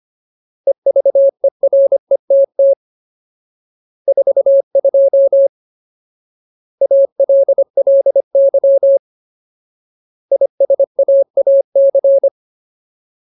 Morse code, heard as a keyed tone, about 25 wpm, 550 Hz. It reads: EVERETT 42 ALLY ISAAC
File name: Ultimate Word List / 25WPM Ultimate